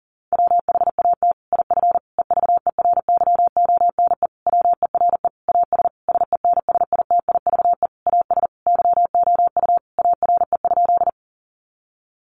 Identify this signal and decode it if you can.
Morse code, heard as a keyed tone, about 37 wpm, 700 Hz. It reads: WHAT IF EVERYONE WERE AS SENSITIVE AS YOU ARE?
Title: Sentences from Top 1400 Words 40wpm